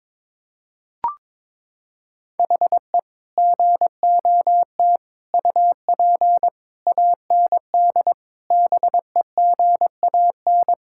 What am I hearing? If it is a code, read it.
Morse code, heard as a keyed tone, about 22 wpm, 700 Hz. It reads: HE GOT UP AND BEGAN
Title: Sentences from Top 1600 Words 22wpm